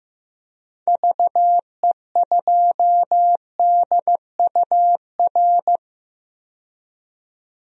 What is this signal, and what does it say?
Morse code, heard as a keyed tone, about 15 wpm, 700 Hz. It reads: VE2DUR
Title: DXCC-100 Call Signs - 6 Characters - 15wpm